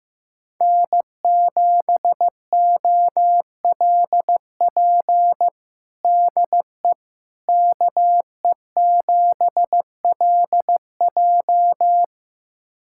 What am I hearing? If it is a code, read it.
Morse code, heard as a keyed tone, about 15 wpm, 700 Hz. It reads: N7OLP DE KE7LJ
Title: USA Callsign DE Callsign - 5 Characters - 15wpm